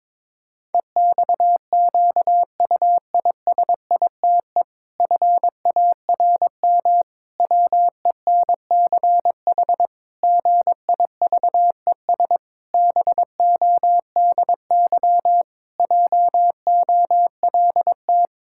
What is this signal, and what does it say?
Morse code, heard as a keyed tone, about 22 wpm, 700 Hz. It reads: EXQUISITE FARM WENCH GIVES BODY JOLT
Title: Pangram Challenge v2 - 22wpm